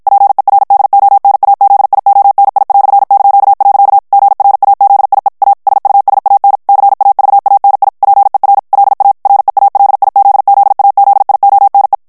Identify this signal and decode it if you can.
Morse code, heard as a keyed tone, about 54 wpm, 800 Hz. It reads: PERDONAZIONI"). GRAZIE A SUSAN CAVANI PER LA REALIZZAZIONE
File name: morse mp3 / ITALIEN texte clair 1 vitesse 50 mots minute